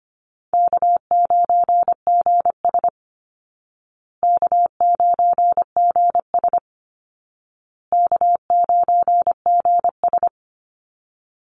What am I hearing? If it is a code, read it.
Morse code, heard as a keyed tone, about 25 wpm, 700 Hz. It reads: K9GS K9GS K9GS
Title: Easy Call Signs Repeated 3x - 25wpm